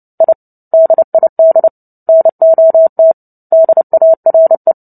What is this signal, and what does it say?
Morse code, heard as a keyed tone, about 30 wpm, 650 Hz. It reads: I DID NOT DARE
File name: Book Two / 03-Chapter 3 - 29 WPM